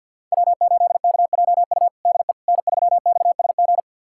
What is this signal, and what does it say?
Morse code, heard as a keyed tone, about 52 wpm, 700 Hz. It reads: W8KJU DE N3XSG
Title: USA Callsign DE Callsign - 5 Characters - 50wpm